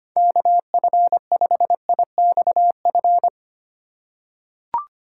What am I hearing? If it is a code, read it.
Morse code, heard as a keyed tone, about 25 wpm, 700 Hz. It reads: KF5IXF
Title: USA Call Signs - 6 Characters - 4x Character Spacing - 25wpm